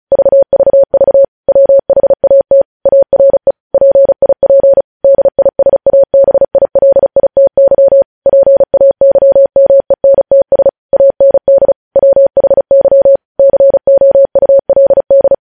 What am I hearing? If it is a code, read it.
Morse code, heard as a keyed tone, about 36 wpm, 550 Hz. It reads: VVV WHAT ARE PIP DISABILITY PAYMENTS AND WHY COULD
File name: podcast / News headlines-1x-35x35